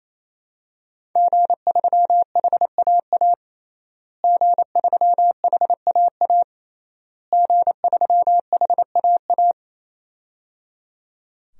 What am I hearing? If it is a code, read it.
Morse code, heard as a keyed tone, about 28 wpm, 700 Hz. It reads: G3HAA G3HAA G3HAA
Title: DXCC-100 Call Signs - 5 Characters - Repeated 3x - 28wpm